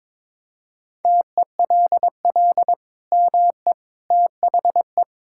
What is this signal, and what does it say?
Morse code, heard as a keyed tone, about 22 wpm, 700 Hz. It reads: TELL ME THE
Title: Sentences from Top 1000 Words - Review Part 5 - 22wpm